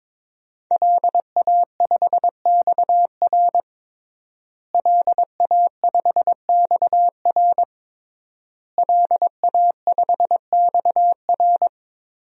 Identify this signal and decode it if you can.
Morse code, heard as a keyed tone, about 22 wpm, 700 Hz. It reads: LA5XR LA5XR LA5XR
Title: DXCC-100 Call Signs - 5 Characters - Repeated 3x - 22wpm